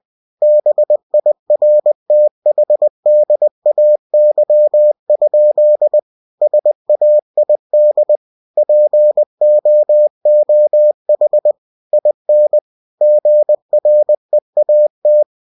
Morse code, heard as a keyed tone, about 20 wpm, 600 Hz. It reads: BIRTHDAY? SAID POOH IN GREAT
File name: cw-books / winnie-the-pooh / winnie-20wpm-page-0046